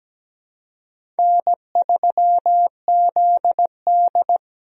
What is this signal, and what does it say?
Morse code, heard as a keyed tone, about 17 wpm, 700 Hz. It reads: N3ZD
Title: USA Call Signs - 4 Characters - 17wpm